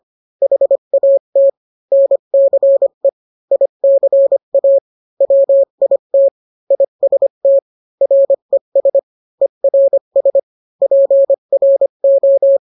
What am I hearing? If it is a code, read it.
Morse code, heard as a keyed tone, about 25 wpm, 550 Hz. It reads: HAT NCE ICA WIT IST RES ERS PRO